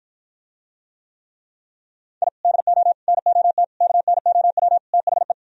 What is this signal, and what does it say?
Morse code, heard as a keyed tone, about 52 wpm, 700 Hz. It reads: I DO NOT KNOW THE